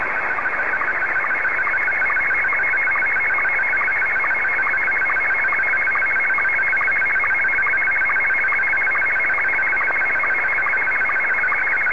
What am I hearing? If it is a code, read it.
SSTV (slow-scan television), Scottie 1.